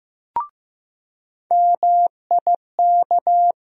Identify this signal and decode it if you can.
Morse code, heard as a keyed tone, about 15 wpm, 700 Hz. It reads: MIK
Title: USA Call Sign Postfixes - 3 Letters 15wpm